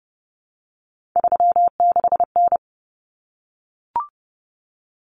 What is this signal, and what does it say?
Morse code, heard as a keyed tone, about 30 wpm, 700 Hz. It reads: 36N
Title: Common Cut Numbers - 3-Digit 30wpm